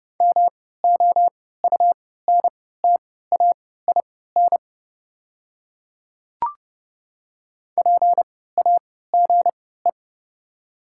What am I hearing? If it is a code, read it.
Morse code, heard as a keyed tone, about 31 wpm, spaced as for 17 wpm, 700 Hz. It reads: MOUNTAIN PAGE
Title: 3x Character Spacing - Top 500 Words - 30wpm